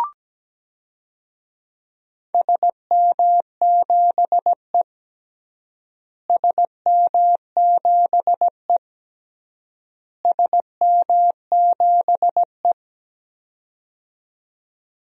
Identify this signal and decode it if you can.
Morse code, heard as a keyed tone, about 17 wpm, 700 Hz. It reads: SM7E SM7E SM7E
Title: DXCC-100 Call Signs - 4 Characters - Repeated 3x - 17wpm